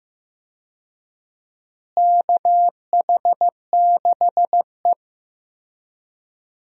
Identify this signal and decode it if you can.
Morse code, heard as a keyed tone, about 15 wpm, 700 Hz. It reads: KH6E
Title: DXCC-100 Call Signs - 4 Characters - Repeated 2x - 15wpm